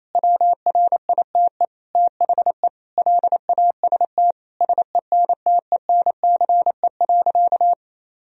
Morse code, heard as a keyed tone, about 28 wpm, 700 Hz. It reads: WRITE THE LAST SENTENCE.
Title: Sentences from Top 1000 Words - Review Part 1 - 28wpm